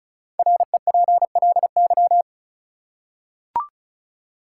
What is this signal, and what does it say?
Morse code, heard as a keyed tone, about 36 wpm, 700 Hz. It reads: REPLY
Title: Five Letter Words - 35wpm